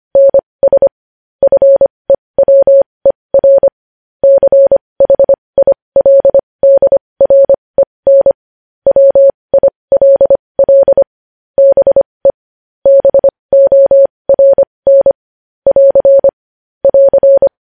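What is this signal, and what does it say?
Morse code, heard as a keyed tone, about 25 wpm, spaced as for 20 wpm, 550 Hz. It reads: NS FEWER CHILDREN WILL BE BORN <AR> <AR>
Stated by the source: Morse code news headlines sent at 20.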